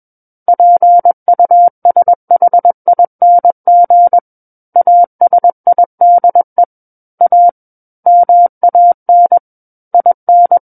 Morse code, heard as a keyed tone, about 21 wpm, 700 Hz. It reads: PUSHING ASIDE A MAN IN
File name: Book One / 12-Chapter 12 - 21 WPM